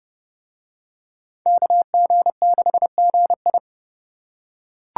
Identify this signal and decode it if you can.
Morse code, heard as a keyed tone, about 28 wpm, 700 Hz. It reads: KG6GI